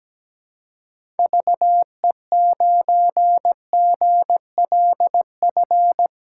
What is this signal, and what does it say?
Morse code, heard as a keyed tone, about 17 wpm, 700 Hz. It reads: VE9GLF
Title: K1USN SST QSO Practice - 17wpm